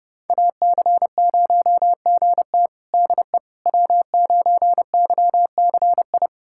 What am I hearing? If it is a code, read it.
Morse code, heard as a keyed tone, about 32 wpm, 700 Hz. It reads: AC0GT DE W9YCI
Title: USA Callsign DE Callsign - 5 Characters - 30wpm